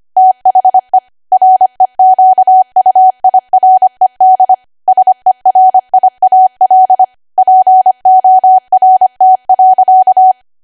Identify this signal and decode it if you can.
Morse code, heard as a keyed tone, about 25 wpm, 750 Hz. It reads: THE REQUIRED SERIAL PORT.
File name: Archive / 25 WPM / 25WPM